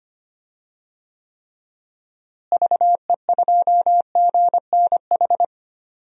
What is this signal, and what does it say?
Morse code, heard as a keyed tone, about 25 wpm, 700 Hz. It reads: VE2GNH
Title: DXCC-100 Call Signs - 6 Characters - Repeated 3x - 25wpm